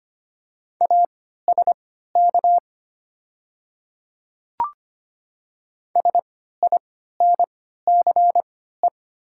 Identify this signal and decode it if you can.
Morse code, heard as a keyed tone, about 25 wpm, spaced as for 14 wpm, 700 Hz. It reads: ASK SINCE
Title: 3x Character Spacing - Top 500 Words - 25wpm